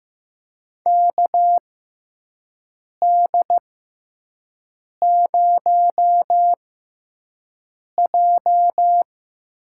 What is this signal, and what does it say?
Morse code, heard as a keyed tone, about 15 wpm, spaced as for 5 wpm, 700 Hz. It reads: KD0J
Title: USA Call Signs - 6 Characters - 6x Character Spacing - 15wpm